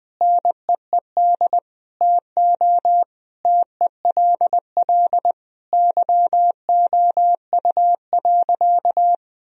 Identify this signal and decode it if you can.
Morse code, heard as a keyed tone, about 20 wpm, 700 Hz. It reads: NEED TO TELL YOU.
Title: Sentences from Top 1000 Words - Review Part 1 - 20wpm